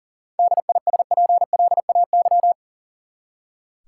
Morse code, heard as a keyed tone, about 40 wpm, 700 Hz. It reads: DISPLAY
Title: Seven Letter Words - 40wpm